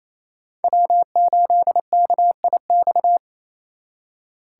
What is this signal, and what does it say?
Morse code, heard as a keyed tone, about 33 wpm, 700 Hz. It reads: W8KIX
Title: Speed Racing - USA 1x3 Call Signs 28wpm